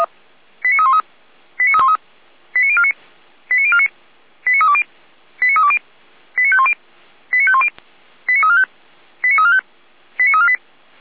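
Five-tone selective call (ZVEI), heard as quick runs of five tones, tones of about 70 ms. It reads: 89313 89313 89049 89049 89320 89320 89510 89510 89345 89345 89348
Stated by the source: ZVEI five-tone call received on SDR, feuerwehr-orlamuende (fire brigade Orlamünde).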